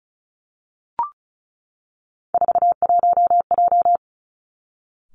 Morse code, heard as a keyed tone, about 36 wpm, 700 Hz. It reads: 41J
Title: Three Challenging Letter-Number Combinations 35wpm